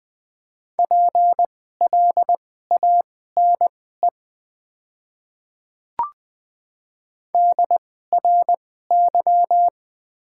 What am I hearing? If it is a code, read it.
Morse code, heard as a keyed tone, about 20 wpm, spaced as for 15 wpm, 690 Hz. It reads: PLANE DRY
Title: 2x Character Spacing - Top 500 Words - 20wpm